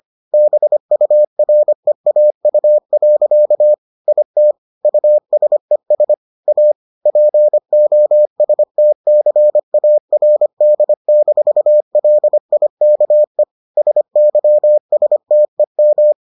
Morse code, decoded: BUREAU. IT USES A POSTCARD-LIKE SYSTEM

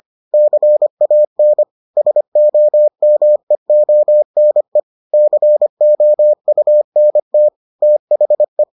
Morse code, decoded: CAN SOMEONE COUNT THE

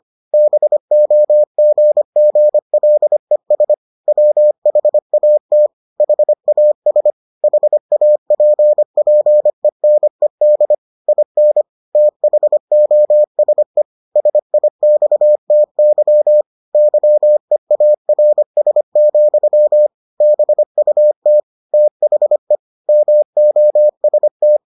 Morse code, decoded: BOGGLES WHAT HAS HAPPENED IN THOSE SIXTY YEARS, BUT THE MOST

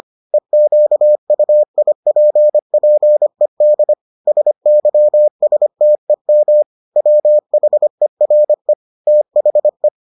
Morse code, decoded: EQUIPPED SYSTEM WHERE THE

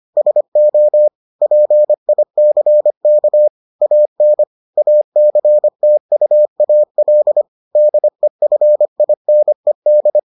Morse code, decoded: SO PICK AN ACTUAL DEFINED

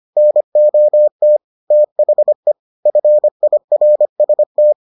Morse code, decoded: NOT THE FIRST

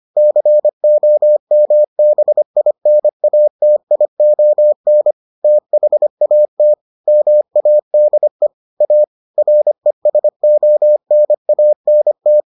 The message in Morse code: COMBINATION THAT MADE A RESONANT

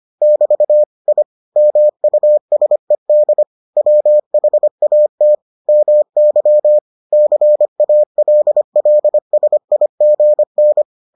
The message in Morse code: <BT> I MUSED WHAT MY CALLSIGN

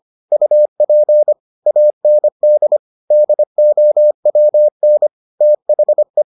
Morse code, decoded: UP AND DOWN THE

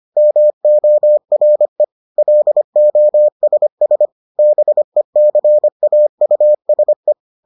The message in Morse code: MORE LOSS BECAUSE

